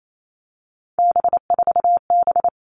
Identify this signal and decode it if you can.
Morse code, heard as a keyed tone, about 28 wpm, 700 Hz. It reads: B4B
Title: All Cut Numbers - 3-Digit 28wpm